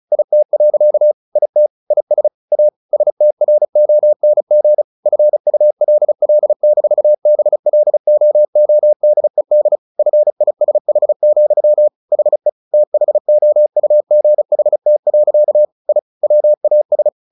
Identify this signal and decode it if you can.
Morse code, heard as a keyed tone, about 34 wpm, 600 Hz. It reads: IT. IT IS A STRONG FULL-BLOODED FISH, HE THOUGHT. I WAS
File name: the old man and the sea / the old man and the sea 1 35wpm